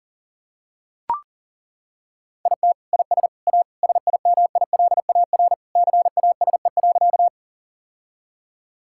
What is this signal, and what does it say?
Morse code, heard as a keyed tone, about 40 wpm, 700 Hz. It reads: IT IS A SIMILAR CASE.
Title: Sentences from Top 1000 Words - Review Part 5 - 40wpm